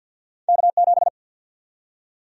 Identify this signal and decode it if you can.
Morse code, heard as a keyed tone, about 51 wpm, 700 Hz. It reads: K7
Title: USA Call Sign Prefixes - 50wpm